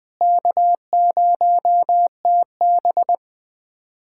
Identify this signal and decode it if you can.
Morse code, heard as a keyed tone, about 20 wpm, 700 Hz. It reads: K0TB
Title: USA Callsign DE Callsign - 4 Characters - 20wpm